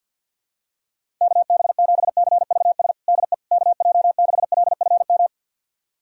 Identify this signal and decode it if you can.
Morse code, heard as a keyed tone, about 47 wpm, 700 Hz. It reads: KB7CVS DE KJ6LFM
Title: USA Callsign DE Callsign - 6 Characters - 50wpm